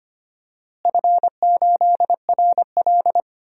Morse code, heard as a keyed tone, about 25 wpm, 700 Hz. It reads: F8RL
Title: DXCC-100 Call Signs - 4 Characters - Repeated 2x - 25wpm